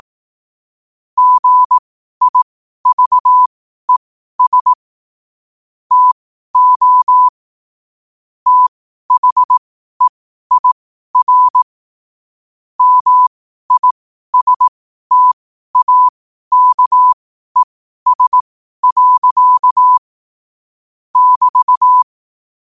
Morse code, heard as a keyed tone, about 18 wpm, spaced as for 12 wpm, 1000 Hz. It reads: GIVES TO THEIR MISTAKES. <BT>
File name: qotdfiles / 12wpm